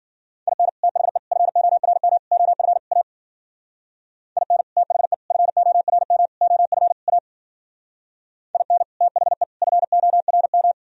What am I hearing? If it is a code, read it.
Morse code, heard as a keyed tone, about 52 wpm, 700 Hz. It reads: IN THE FORM OF A IN THE FORM OF A IN THE FORM